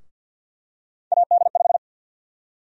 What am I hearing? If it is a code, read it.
Morse code, heard as a keyed tone, about 47 wpm, 700 Hz. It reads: AD5
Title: USA Call Sign Prefixes - Repeated 2x - 50wpm